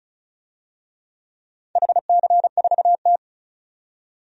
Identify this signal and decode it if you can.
Morse code, heard as a keyed tone, about 36 wpm, 690 Hz. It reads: HC4T
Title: DXCC-100 Call Signs - 4 Characters - 35wpm